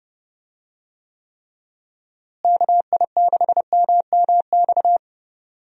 Morse code, heard as a keyed tone, about 30 wpm, 700 Hz. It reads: KI6MMX
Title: USA Call Signs - 6 Characters - Repeated 3x - 30wpm